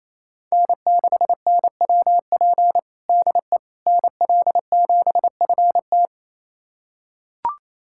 Morse code, heard as a keyed tone, about 28 wpm, 700 Hz. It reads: N6NWP DE NL7FT
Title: USA Callsign DE Callsign - 5 Characters - 28wpm